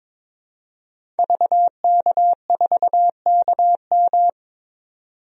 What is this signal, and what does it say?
Morse code, heard as a keyed tone, about 21 wpm, 700 Hz. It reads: VK4KM
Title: DXCC-100 Call Signs - 5 Characters - 22wpm